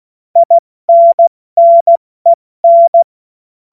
Morse code, heard as a keyed tone, about 16 wpm, 680 Hz. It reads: INNEN
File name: Morsekurs / Witze / Witz 05. Oktober 2021, 70 F80